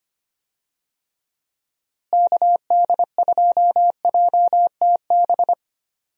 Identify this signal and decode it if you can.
Morse code, heard as a keyed tone, about 25 wpm, 700 Hz. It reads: KD2JTB
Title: USA Call Signs - 6 Characters - Repeated 3x - 25wpm